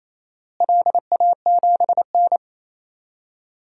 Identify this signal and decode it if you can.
Morse code, heard as a keyed tone, about 28 wpm, 700 Hz. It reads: LA7N
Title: DXCC-100 Call Signs - 4 Characters - Repeated 3x - 28wpm